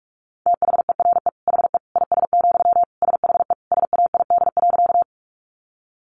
Morse code, heard as a keyed tone, about 44 wpm, 700 Hz. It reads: THERE HE IS, SHE SAID.
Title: Sentences from Top 100 Words 45wpm